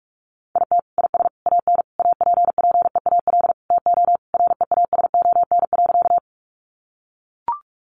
Morse code, heard as a keyed tone, about 44 wpm, 700 Hz. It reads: IT IS AN APPEAL TO REASON.
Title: Sentences from Top 1300 Words 45wpm